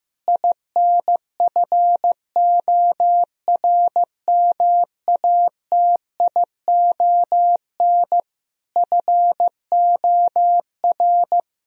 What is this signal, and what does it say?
Morse code, heard as a keyed tone, about 15 wpm, 700 Hz. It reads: INFORMATION FOR
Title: Sentences from Top 1000 Words - Review Part 3 - 15wpm